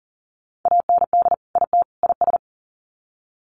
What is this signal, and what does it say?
Morse code, heard as a keyed tone, about 40 wpm, 700 Hz. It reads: AND IT IS
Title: Sets of 3 Words 40wpm